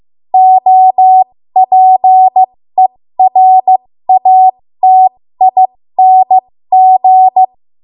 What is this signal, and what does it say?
Morse code, heard as a keyed tone, about 15 wpm, spaced as for 13 wpm, 750 Hz. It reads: OPERATING
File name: Archive / 13 WPM / 13WPM